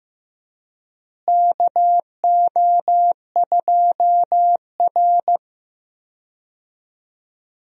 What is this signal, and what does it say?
Morse code, heard as a keyed tone, about 15 wpm, 700 Hz. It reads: KO2R
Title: USA Call Signs - 4 Characters - Repeated 3x - 15wpm